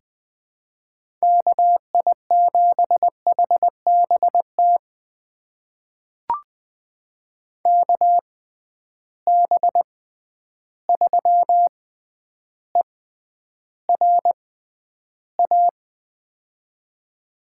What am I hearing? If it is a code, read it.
Morse code, heard as a keyed tone, about 20 wpm, spaced as for 10 wpm, 700 Hz. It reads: KI7HBT KB3ERA
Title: USA Call Signs - 6 Characters - 6x Character Spacing - 20wpm